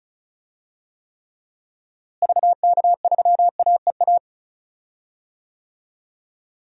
Morse code, heard as a keyed tone, about 36 wpm, 700 Hz. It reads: VK3AEA
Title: DXCC-100 Call Signs - 6 Characters - Repeated 2x - 35wpm